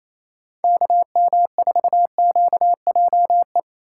Morse code, heard as a keyed tone, about 27 wpm, 700 Hz. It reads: KM4QJE
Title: USA Call Signs - 6 Characters - 3x Character Spacing - 28wpm